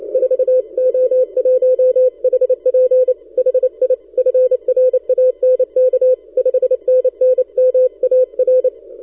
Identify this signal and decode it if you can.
Morse code, heard as a keyed tone, about 28 wpm, 510 Hz. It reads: 4O1HP HI FRANK 5NN MAR
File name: pa0oos / recall2013 / vy2zm 160test